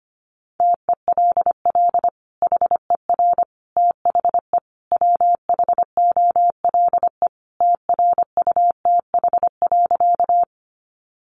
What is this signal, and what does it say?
Morse code, heard as a keyed tone, about 25 wpm, 700 Hz. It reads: TELL HER THE WHOLE TRUTH.
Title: Sentences from Top 800 Words 25wpm